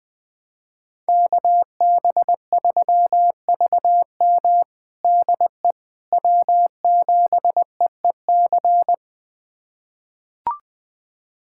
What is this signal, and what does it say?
Morse code, heard as a keyed tone, about 21 wpm, 700 Hz. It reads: KB3VM DE W7EEC
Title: USA Callsign DE Callsign - 5 Characters - 20wpm